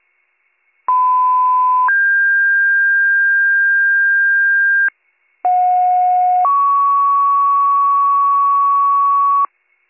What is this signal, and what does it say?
Two-tone page, heard as a one-second tone, then a three-second tone, about 1010 then 1600 Hz, about 730 then 1120 Hz. It is heard from 0.9 to 9.5 s.